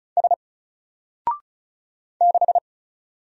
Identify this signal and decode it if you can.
Morse code, heard as a keyed tone, about 36 wpm, 700 Hz. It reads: S 6
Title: Single Letter-Number Alt 35wpm